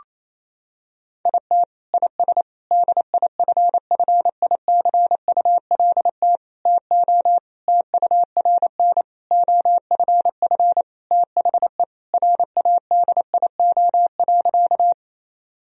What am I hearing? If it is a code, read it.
Morse code, heard as a keyed tone, about 28 wpm, 700 Hz. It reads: IT IS DIFFICULT TO TURN OFF THE RADIO.
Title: Sentences from Top 1000 Words - Review Part 3 - 28wpm